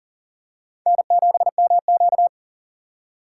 Morse code, heard as a keyed tone, about 40 wpm, 700 Hz. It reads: N7MQ
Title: USA Call Signs - 4 Characters - 6x Character Spacing - 40wpm